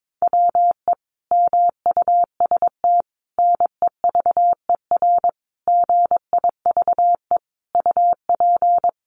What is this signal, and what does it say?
Morse code, heard as a keyed tone, about 22 wpm, 700 Hz. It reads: WE MUST NEVER GIVE UP
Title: Sentences from Top 1500 Words 22wpm